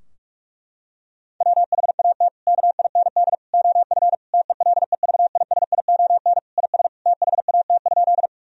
Morse code, heard as a keyed tone, about 41 wpm, 700 Hz. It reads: WHAT KIND OF TELEVISION IS THAT?
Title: Sentences from Top 1000 Words - Review Part 4 - 45wpm